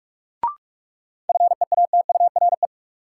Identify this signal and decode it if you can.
Morse code, heard as a keyed tone, about 46 wpm, 700 Hz. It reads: FEATURE
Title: Sentences from Top 800 Words - New Word Drill - 45wpm